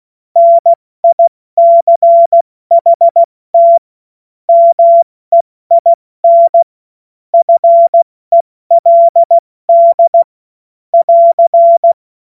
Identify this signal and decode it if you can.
Morse code, heard as a keyed tone, about 16 wpm, 680 Hz. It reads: NICHT MEIN FELD <AR>
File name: Morsekurs / Witze / Witz 29 April 2025 70 F80